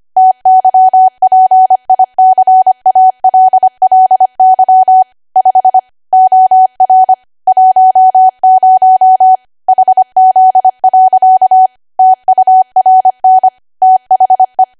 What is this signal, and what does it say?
Morse code, heard as a keyed tone, about 25 wpm, 750 Hz. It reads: TYPICALLY 5 OR 10 HZ. TURN THE
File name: Archive / 25 WPM / 25WPM